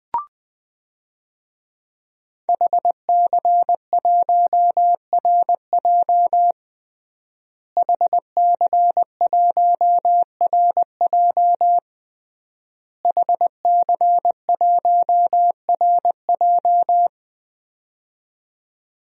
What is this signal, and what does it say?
Morse code, heard as a keyed tone, about 20 wpm, 700 Hz. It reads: HC1RJ HC1RJ HC1RJ
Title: DXCC-100 Call Signs - 5 Characters - Repeated 3x - 20wpm